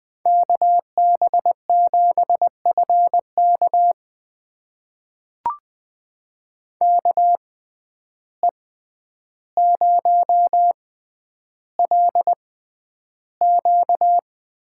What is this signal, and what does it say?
Morse code, heard as a keyed tone, about 20 wpm, spaced as for 10 wpm, 700 Hz. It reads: KB7FK KE0LQ
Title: USA Call Signs - 5 Characters - 6x Character Spacing - 20wpm